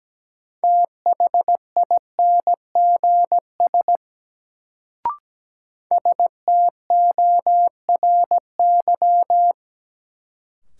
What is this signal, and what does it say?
Morse code, heard as a keyed tone, about 17 wpm, 700 Hz. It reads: THINGS STORY
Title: Sets of 3 Words - New Word Drill - 17wpm